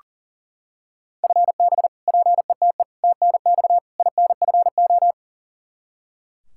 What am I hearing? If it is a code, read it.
Morse code, heard as a keyed tone, about 40 wpm, 700 Hz. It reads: FB PETE TNX INFO
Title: Practice QSO - Exchange 3 - More Info - 40wpm